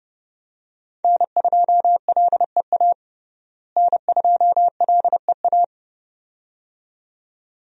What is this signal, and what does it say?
Morse code, heard as a keyed tone, about 31 wpm, 700 Hz. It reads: N2LEA N2LEA
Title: USA Call Signs - 5 Characters - Repeated 2x - 30wpm